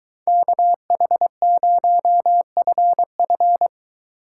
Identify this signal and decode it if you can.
Morse code, heard as a keyed tone, about 23 wpm, 700 Hz. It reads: KH0FF